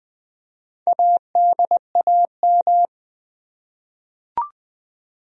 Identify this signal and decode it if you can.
Morse code, heard as a keyed tone, about 20 wpm, 700 Hz. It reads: ADAM
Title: USA Common Names - Max 4 Letters - 20wpm